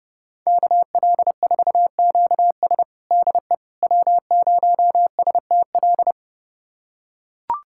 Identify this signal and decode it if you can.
Morse code, heard as a keyed tone, about 30 wpm, 700 Hz. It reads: KL4QS DE W0STL
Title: USA Callsign DE Callsign - 5 Characters - 30wpm